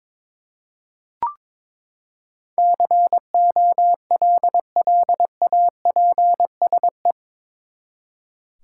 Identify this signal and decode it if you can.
Morse code, heard as a keyed tone, about 22 wpm, 700 Hz. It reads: COLLAPSE
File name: practice / 22wpm / Eight Letter Words - 22wpm